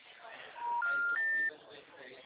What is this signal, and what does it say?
Special information tone, with speech.